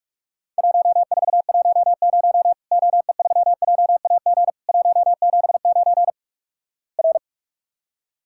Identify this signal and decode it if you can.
Morse code, heard as a keyed tone, about 46 wpm, 700 Hz. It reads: 1410 OE3JAG 179 R
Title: Worked All Europe DX Contest - QTC Exchange - 45wpm